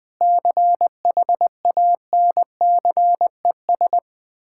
Morse code, heard as a keyed tone, about 21 wpm, 700 Hz. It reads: CHANCES
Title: Sentences with 8-Word Phrases - 20wpm